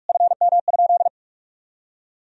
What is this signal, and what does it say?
Morse code, heard as a keyed tone, about 41 wpm, 700 Hz. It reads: FM?
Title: Sentences from Top 200 Words and Abbreviations 45wpm